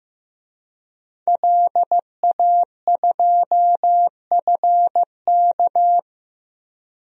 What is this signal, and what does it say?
Morse code, heard as a keyed tone, about 15 wpm, 700 Hz. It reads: LA2FK
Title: DXCC-100 Call Signs - 5 Characters - 15wpm